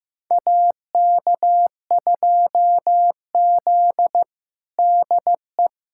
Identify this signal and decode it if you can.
Morse code, heard as a keyed tone, about 15 wpm, 700 Hz. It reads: AK2Z DE
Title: USA Callsign DE Callsign - 4 Characters - 15wpm